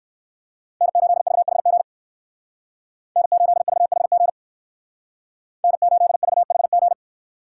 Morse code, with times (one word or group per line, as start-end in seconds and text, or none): 0.8-1.8 N8VHG
3.2-4.3 N8VHG
5.6-6.9 N8VHG